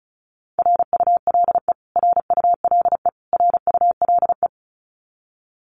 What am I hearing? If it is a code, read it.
Morse code, heard as a keyed tone, about 34 wpm, 700 Hz. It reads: RULE RULE RULE
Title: Top 500 Words Repeated 35wpm